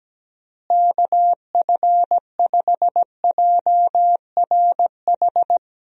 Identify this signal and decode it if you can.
Morse code, heard as a keyed tone, about 17 wpm, 700 Hz. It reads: KF5JRH
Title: USA Call Signs - 6 Characters - Repeated 2x - 17wpm